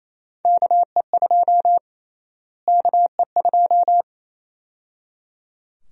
Morse code, heard as a keyed tone, about 28 wpm, 700 Hz. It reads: KE2 KE2
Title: USA Call Sign Prefixes - Repeated 2x - 28wpm